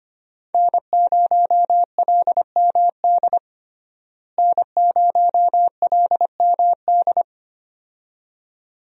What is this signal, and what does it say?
Morse code, heard as a keyed tone, about 25 wpm, 700 Hz. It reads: N0LMD N0LMD
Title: USA Call Signs - 5 Characters - Repeated 2x - 25wpm